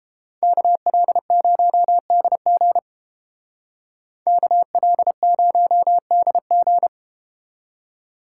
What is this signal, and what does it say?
Morse code, heard as a keyed tone, about 31 wpm, 700 Hz. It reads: KL0DG KL0DG